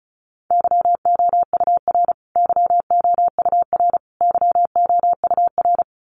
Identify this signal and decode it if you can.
Morse code, heard as a keyed tone, about 34 wpm, 700 Hz. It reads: YOUR YOUR YOUR
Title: Top 500 Words Repeated 35wpm